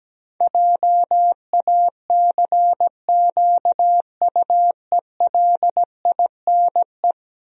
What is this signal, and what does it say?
Morse code, heard as a keyed tone, about 17 wpm, 700 Hz. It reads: JACQUELINE
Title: USA Common Names - 17wpm